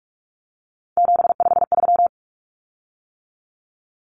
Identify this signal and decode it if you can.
Morse code, heard as a keyed tone, about 44 wpm, 700 Hz. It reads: 753
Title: Common Cut Numbers - 3-Digit 45wpm